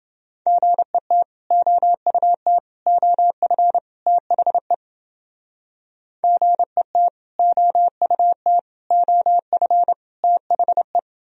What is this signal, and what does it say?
Morse code, heard as a keyed tone, about 31 wpm, 700 Hz. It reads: GET OUT OF THE GET OUT OF THE